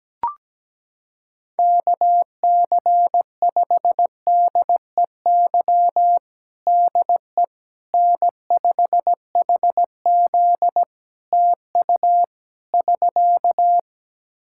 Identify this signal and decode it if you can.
Morse code, heard as a keyed tone, about 17 wpm, 700 Hz. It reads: KC5DEY DE N5HZ TU <SK>
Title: Practice QSO - Exchange 4 - Ending - 17wpm